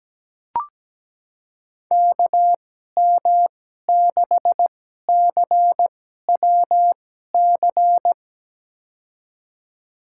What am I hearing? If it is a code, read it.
Morse code, heard as a keyed tone, about 17 wpm, spaced as for 12 wpm, 700 Hz. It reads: KM6CWC
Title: USA Call Signs - 6 Characters - 2x Character Spacing - 17wpm